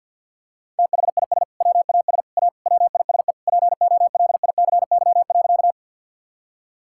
Morse code, heard as a keyed tone, about 50 wpm, 700 Hz. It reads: THIS WAS A WISE POLICY.
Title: Sentences from Top 1600 Words 50wpm